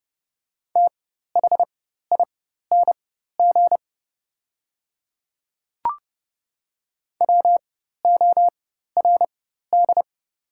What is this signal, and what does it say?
Morse code, heard as a keyed tone, about 30 wpm, spaced as for 14 wpm, 690 Hz. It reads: THING WORD
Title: 4x Character Spacing - Top 100 Words - 30wpm